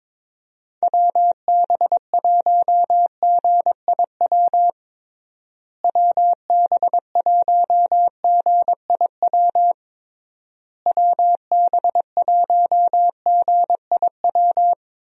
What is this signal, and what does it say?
Morse code, heard as a keyed tone, about 22 wpm, 700 Hz. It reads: WB1GIW WB1GIW WB1GIW
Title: USA Call Signs - 6 Characters - Repeated 3x - 22wpm